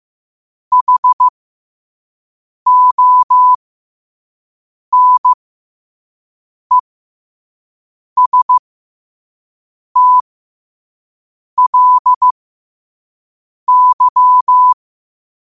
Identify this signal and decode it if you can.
Morse code, heard as a keyed tone, about 15 wpm, spaced as for 5 wpm, 1000 Hz. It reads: HONESTLY
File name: qotdfiles / 5wpm